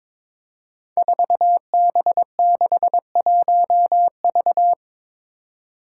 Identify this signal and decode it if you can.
Morse code, heard as a keyed tone, about 21 wpm, 700 Hz. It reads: 4B61V
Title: Five Challenging Letter-Number Combinations 22wpm